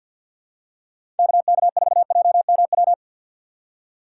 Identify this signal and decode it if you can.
Morse code, heard as a keyed tone, about 47 wpm, 700 Hz. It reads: KK4JMW
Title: USA Call Signs - 6 Characters - 4x Character Spacing - 50wpm